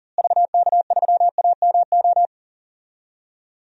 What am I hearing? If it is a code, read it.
Morse code, heard as a keyed tone, about 40 wpm, 700 Hz. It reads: VK3AMO
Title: DXCC-100 Call Signs - 6 Characters - Repeated 3x - 40wpm